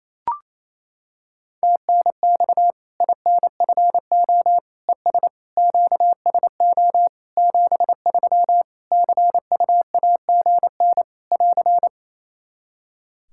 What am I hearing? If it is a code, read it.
Morse code, heard as a keyed tone, about 28 wpm, 700 Hz. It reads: TNX INFO ES QSO 73 CUAGN <AR>
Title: Practice QSO - Exchange 4 - Ending - 28wpm